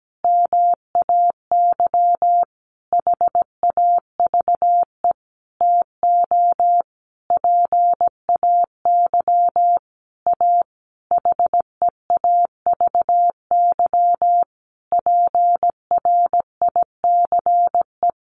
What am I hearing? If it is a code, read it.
Morse code, heard as a keyed tone, about 17 wpm, 700 Hz. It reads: MAY HAVE TO PAY A HEAVY PRICE
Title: Sentences from Top 1100 Words 17wpm